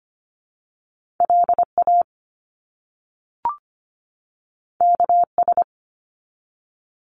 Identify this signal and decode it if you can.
Morse code, heard as a keyed tone, about 25 wpm, 700 Hz. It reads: LA KS
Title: US States 25wpm